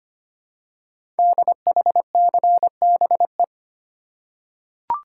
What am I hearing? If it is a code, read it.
Morse code, heard as a keyed tone, about 25 wpm, 700 Hz. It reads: DHCBE
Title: Code Groups - 5 Letters - 25wpm